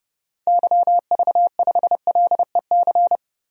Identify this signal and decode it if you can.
Morse code, heard as a keyed tone, about 30 wpm, 700 Hz. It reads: YV5LEC
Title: DXCC-100 Call Signs - 6 Characters - Repeated 3x - 30wpm